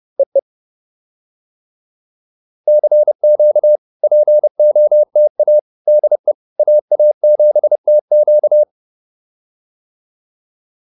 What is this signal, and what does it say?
Morse code, heard as a keyed tone, about 30 wpm, 600 Hz. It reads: EE CQ POTA DE AA7TQ